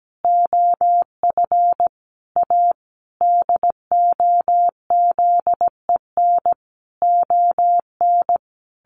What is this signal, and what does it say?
Morse code, heard as a keyed tone, about 17 wpm, 700 Hz. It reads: OF A DOZEN ON
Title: A Sound Of Thunder 17wpm